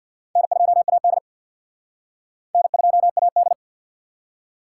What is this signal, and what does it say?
Morse code, heard as a keyed tone, about 55 wpm, 700 Hz. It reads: N2AD N2AD
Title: Speed Racing - USA 1x2 Call Signs 50wpm